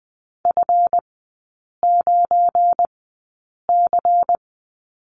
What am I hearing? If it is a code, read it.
Morse code, heard as a keyed tone, about 20 wpm, spaced as for 8 wpm, 700 Hz. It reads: F9C
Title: 172 Lesson 28e - Number 9 - Three Characters 20wpm